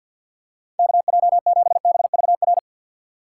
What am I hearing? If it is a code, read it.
Morse code, heard as a keyed tone, about 47 wpm, 700 Hz. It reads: KJ7BVR
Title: USA Call Signs - 6 Characters - 6x Character Spacing - 50wpm